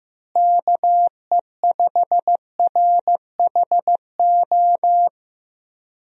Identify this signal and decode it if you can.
Morse code, heard as a keyed tone, about 15 wpm, 700 Hz. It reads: KE5RHO
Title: USA Call Signs - 6 Characters - Repeated 3x - 15wpm